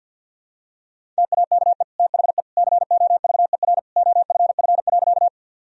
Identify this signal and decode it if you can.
Morse code, heard as a keyed tone, about 51 wpm, 700 Hz. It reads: TAKE THE COVER OFF.
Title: Sentences from Top 1000 Words - Review Part 1 - 50wpm